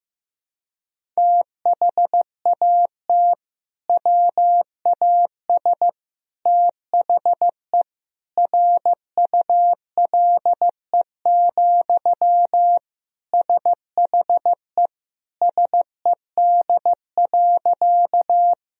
Morse code, heard as a keyed tone, about 15 wpm, 700 Hz. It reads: THAT WAS THE RULE, SHE SED.
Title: Sentences from Top 400 Words and Abbreviations 15wpm